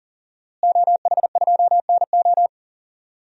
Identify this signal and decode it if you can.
Morse code, heard as a keyed tone, about 40 wpm, 700 Hz. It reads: OH2NO
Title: DXCC-100 Call Signs - 5 Characters - Repeated 2x - 40wpm